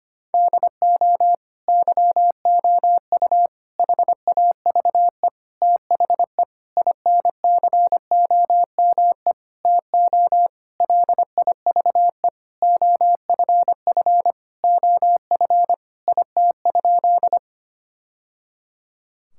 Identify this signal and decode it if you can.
Morse code, heard as a keyed tone, about 25 wpm, 700 Hz. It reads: DO YOU HAVE THE INCOME TO LIVE OFF OF IT?
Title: Sentences from Top 1000 Words - Review Part 5 - 25wpm